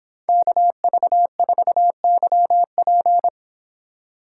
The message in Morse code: KV4YP

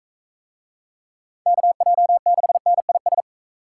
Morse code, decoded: KJ6NIS

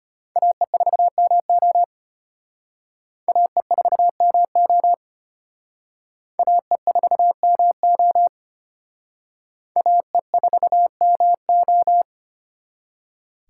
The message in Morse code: AE4MO AE4MO AE4MO AE4MO

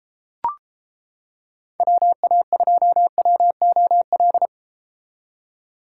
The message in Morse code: WA2WOL

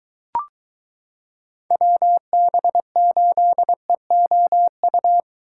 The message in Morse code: WB8EOU

